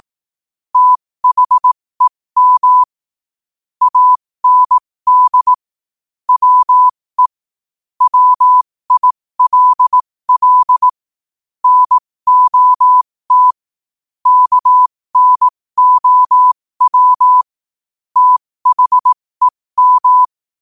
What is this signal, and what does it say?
Morse code, heard as a keyed tone, about 18 wpm, spaced as for 15 wpm, 1000 Hz. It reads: THEM AND WE WILL NOT KNOW THEM
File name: qotdfiles / 15wpm